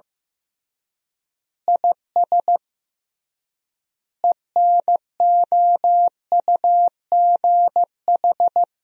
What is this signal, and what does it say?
Morse code, heard as a keyed tone, about 15 wpm, 700 Hz. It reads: IS ENOUGH
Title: 3x Word-Spacing - Sentences from Top 400 Words - 15wpm